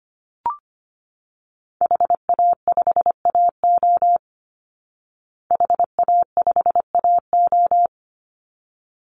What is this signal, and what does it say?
Morse code, heard as a keyed tone, about 25 wpm, 700 Hz. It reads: HA5AO HA5AO
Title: Call Signs Repeated 3x - 25wpm